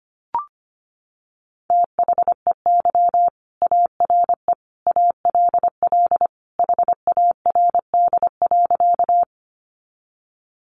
Morse code, heard as a keyed tone, about 25 wpm, 700 Hz. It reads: THEY ARE ALL HARD.
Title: Additional Easy Phrases 25wpm